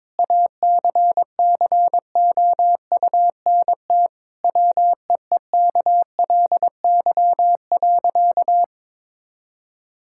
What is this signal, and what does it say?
Morse code, heard as a keyed tone, about 22 wpm, 700 Hz. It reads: ACCOUNT WEEKLY.
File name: Difficult Sentences with Hints - 22wpm